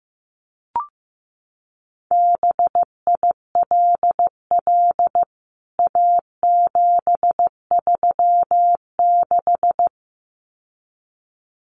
Morse code, heard as a keyed tone, about 15 wpm, 700 Hz. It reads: BILL A736
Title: CWT Exchange 15wpm